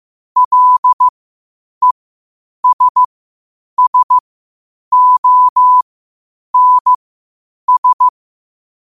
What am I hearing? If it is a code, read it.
Morse code, heard as a keyed tone, about 15 wpm, spaced as for 8 wpm, 1000 Hz. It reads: LESSONS
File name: qotdfiles / 8wpm